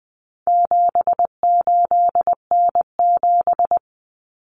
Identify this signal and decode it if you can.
Morse code, heard as a keyed tone, about 20 wpm, 700 Hz. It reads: 78N7
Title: Common Cut Numbers - 4-Digit 20wpm